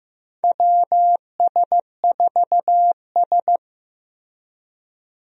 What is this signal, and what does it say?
Morse code, heard as a keyed tone, about 15 wpm, 700 Hz. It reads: WS4S
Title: USA Call Signs - 4 Characters - Repeated 2x - 15wpm